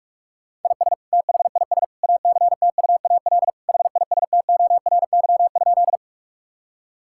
Morse code, heard as a keyed tone, about 46 wpm, 700 Hz. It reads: IS THIS ACTUAL HISTORY?
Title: Sentences from Top 1000 Words - Review Part 5 - 45wpm